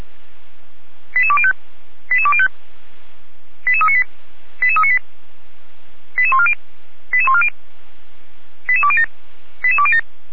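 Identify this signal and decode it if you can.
Five-tone selective call (ZVEI), heard as quick runs of five tones, tones of about 70 ms. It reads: 88205 88205 88398 88398 88140 88140 88207 88207
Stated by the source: ZVEI five-tone selective call 88205 received by SDR, archived on the Feuerwehr Orlamünde fire brigade server.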